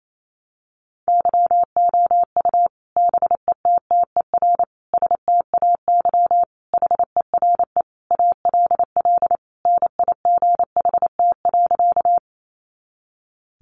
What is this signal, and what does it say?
Morse code, heard as a keyed tone, about 28 wpm, 700 Hz. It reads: YOU BETTER STAY HERE ALL NIGHT.
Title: Sentences from Top 500 Words 28wpm